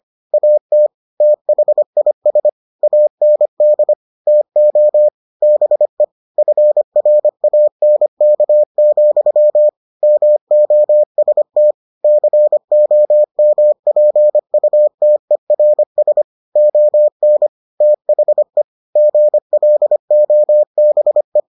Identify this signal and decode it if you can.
Morse code, heard as a keyed tone, about 25 wpm, 600 Hz. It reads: AT THIS AND TO BE FRANK, MOST COMPUTERS ON THE GLOBE